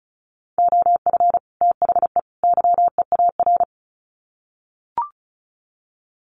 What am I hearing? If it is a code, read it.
Morse code, heard as a keyed tone, about 34 wpm, 700 Hz. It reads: OF THE YEAR
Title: Sets of 3 Words 35wpm